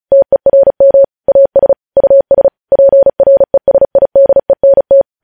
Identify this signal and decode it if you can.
Morse code, heard as a keyed tone, about 36 wpm, 550 Hz. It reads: TERM AS US PRESIDENT
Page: News Headlines in Morse Code at 35 WPM